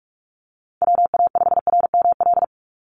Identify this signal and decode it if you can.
Morse code, heard as a keyed tone, about 44 wpm, 700 Hz. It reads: WA5RML
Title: Call Signs - 45wpm